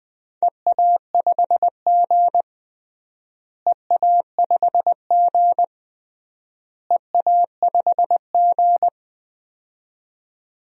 Morse code, heard as a keyed tone, about 20 wpm, 700 Hz. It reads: EA5G EA5G EA5G
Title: DXCC-100 Call Signs - 4 Characters - Repeated 3x - 20wpm